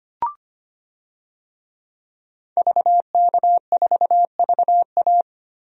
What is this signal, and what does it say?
Morse code, heard as a keyed tone, about 25 wpm, 700 Hz. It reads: VK4VA
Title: DXCC-100 Call Signs - 5 Characters - Repeated 3x - 25wpm